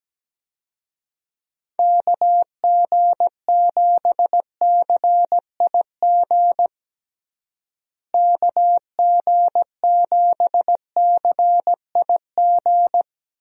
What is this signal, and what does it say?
Morse code, heard as a keyed tone, about 17 wpm, 700 Hz. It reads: KG7CIG KG7CIG
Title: USA Call Signs - 6 Characters - Repeated 2x - 17wpm